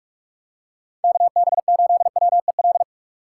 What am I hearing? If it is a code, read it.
Morse code, heard as a keyed tone, about 42 wpm, 700 Hz. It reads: KB8WEL
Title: USA Call Signs - 2x Character Spacing - 45wpm